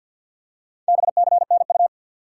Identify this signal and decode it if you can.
Morse code, heard as a keyed tone, about 53 wpm, 700 Hz. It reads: BCNU
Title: Top 90 QSO Elements 50wpm